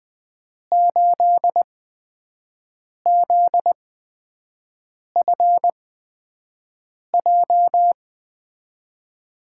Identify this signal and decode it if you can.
Morse code, heard as a keyed tone, about 20 wpm, spaced as for 5 wpm, 700 Hz. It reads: 8ZFJ
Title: USA Call Signs - 6 Characters - 8x Character Spacing - 20wpm